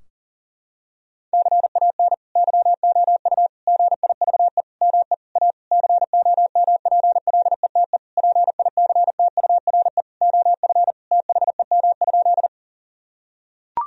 Morse code, heard as a keyed tone, about 38 wpm, 700 Hz. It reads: CAN YOU GIVE ME A COMPLETE PICTURE OF THEM?
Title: Sentences from Top 1000 Words - Review Part 2 - 40wpm